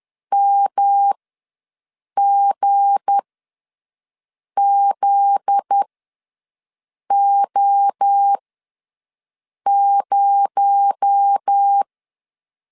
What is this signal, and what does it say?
Morse code, heard as a keyed tone, about 10 wpm, spaced as for 5 wpm, 800 Hz. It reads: MGZO0
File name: LEZIONI / 7 GF9Z / 7 GF9Z ws 10wpm 3919